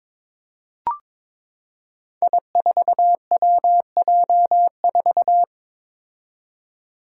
Morse code, heard as a keyed tone, about 22 wpm, 700 Hz. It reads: I4WJ4
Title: Five Challenging Letter-Number Combinations 22wpm